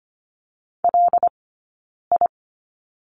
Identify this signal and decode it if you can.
Morse code, heard as a keyed tone, about 25 wpm, spaced as for 9 wpm, 690 Hz. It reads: LI
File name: course / 25:10wpm / 092 Lesson 16d - Letter c - Two Characters 25wpm